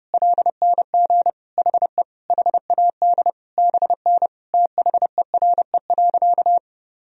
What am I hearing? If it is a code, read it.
Morse code, heard as a keyed tone, about 30 wpm, 700 Hz. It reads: LNG HE HAD BN THERE.
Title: Sentences from Top 300 Words and Abbreviations 30wpm